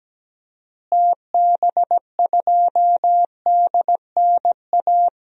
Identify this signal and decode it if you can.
Morse code, heard as a keyed tone, about 17 wpm, 700 Hz. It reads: TB2DNA
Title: DXCC-100 Call Signs - 6 Characters - Repeated 2x - 17wpm